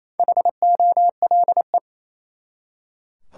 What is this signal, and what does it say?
Morse code, heard as a keyed tone, about 27 wpm, 700 Hz. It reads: HOLE
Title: Four Letter Words - 28wpm